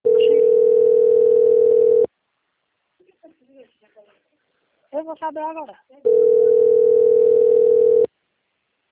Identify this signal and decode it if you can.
Ringback tone (North America), 440 + 480 Hz, with speech.
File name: out- - - - - .42674